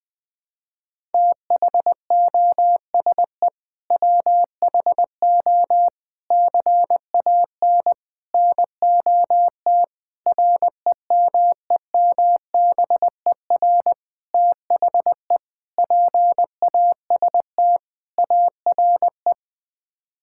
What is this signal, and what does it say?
Morse code, heard as a keyed tone, about 20 wpm, 700 Hz. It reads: THOSE WHO CAN NOT REMEMBER THE PAST ARE
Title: Sets of 8 Words - 20wpm